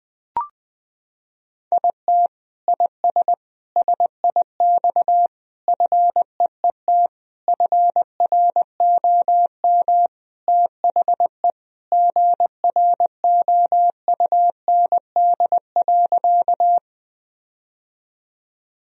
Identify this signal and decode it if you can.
Morse code, heard as a keyed tone, about 20 wpm, 700 Hz. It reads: IT IS SIX FEET FROM THE GROUND.
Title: Sentences from Top 1000 Words - Review Part 2 - 20wpm